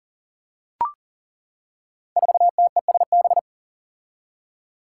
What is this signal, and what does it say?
Morse code, heard as a keyed tone, about 40 wpm, 700 Hz. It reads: 4TESB
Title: Five Challenging Letter-Number Combinations 40wpm